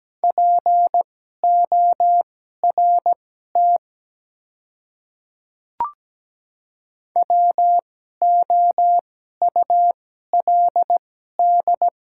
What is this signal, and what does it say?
Morse code, heard as a keyed tone, about 17 wpm, spaced as for 12 wpm, 700 Hz. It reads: PORT WOULD
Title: 2x Character Spacing - Top 200 Words - 17wpm